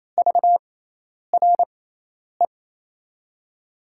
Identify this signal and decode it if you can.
Morse code, heard as a keyed tone, about 28 wpm, spaced as for 10 wpm, 700 Hz. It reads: VRE
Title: USA Call Signs - 6 Characters - 6x Character Spacing - 28wpm